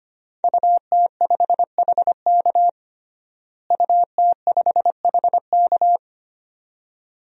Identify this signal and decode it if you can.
Morse code, heard as a keyed tone, about 25 wpm, 700 Hz. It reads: UT5HK UT5HK
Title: DXCC-100 Call Signs - 5 Characters - Repeated 2x - 25wpm